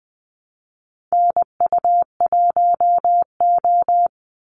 Morse code, heard as a keyed tone, about 20 wpm, 700 Hz. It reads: NU1O
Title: Call Signs Repeated 2x - 20wpm